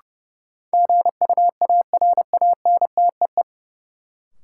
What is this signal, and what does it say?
Morse code, heard as a keyed tone, about 32 wpm, 700 Hz. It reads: GUARANTEE
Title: Sentences from Top 1600 Words - New Word Drill Alt - 30wpm